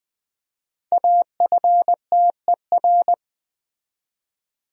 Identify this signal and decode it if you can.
Morse code, heard as a keyed tone, about 20 wpm, 700 Hz. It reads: AFTER
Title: Speed-Racing Top 200 Words 17wpm